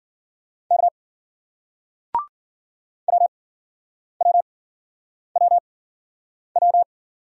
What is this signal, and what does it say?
Morse code, heard as a keyed tone, about 53 wpm, spaced as for 9 wpm, 700 Hz. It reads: K WWWW